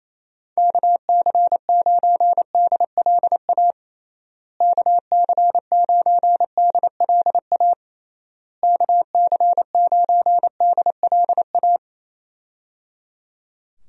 Morse code, heard as a keyed tone, about 27 wpm, 700 Hz. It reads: KC9DLA KC9DLA KC9DLA
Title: USA Call Signs - 6 Characters - Repeated 3x - 28wpm